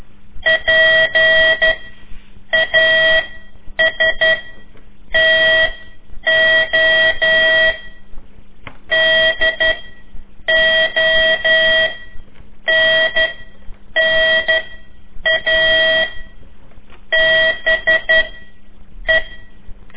Morse code, heard as a keyed tone, about 12 wpm, 1950 Hz, with noise. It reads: P A S T O D O N N A B E
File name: ik1weg2 / CORSI CW / lez4-3